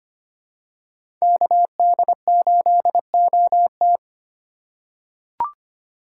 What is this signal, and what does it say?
Morse code, heard as a keyed tone, about 25 wpm, 700 Hz. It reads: KD8OT
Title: USA Call Signs - 4x Character Spacing - 25wpm